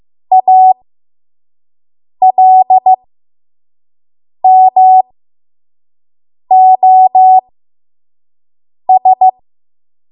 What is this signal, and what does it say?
Morse code, heard as a keyed tone, about 15 wpm, spaced as for 5 wpm, 750 Hz. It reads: ALMOS
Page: 5 WPM Code Archive